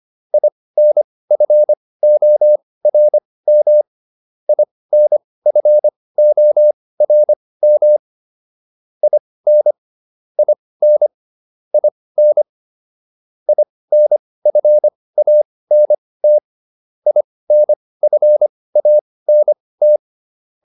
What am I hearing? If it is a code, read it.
Morse code, heard as a keyed tone, about 25 wpm, spaced as for 18 wpm, 600 Hz. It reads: INFORM INFORM IN IN IN INFANT INFANT